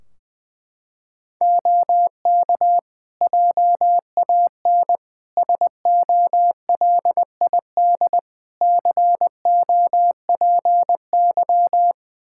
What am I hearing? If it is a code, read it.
Morse code, heard as a keyed tone, about 20 wpm, 700 Hz. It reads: OK JAN SOLID COPY
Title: K1USN SST QSO Practice 20wpm